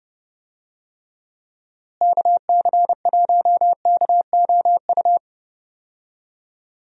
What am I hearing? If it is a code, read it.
Morse code, heard as a keyed tone, about 31 wpm, 700 Hz. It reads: KC1KOU
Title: USA Call Signs - 6 Characters - Repeated 3x - 30wpm